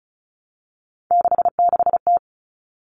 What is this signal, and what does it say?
Morse code, heard as a keyed tone, about 34 wpm, 700 Hz. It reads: 66T
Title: Common Cut Numbers - 3-Digit 35wpm